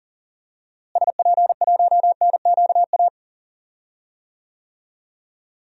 Morse code, heard as a keyed tone, about 40 wpm, 700 Hz. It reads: SP1NQA
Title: DXCC-100 Call Signs - 6 Characters - Repeated 3x - 40wpm